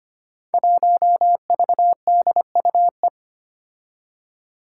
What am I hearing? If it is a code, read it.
Morse code, heard as a keyed tone, about 25 wpm, 700 Hz. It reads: 1VDUE
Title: Five Challenging Letter-Number Combinations 25wpm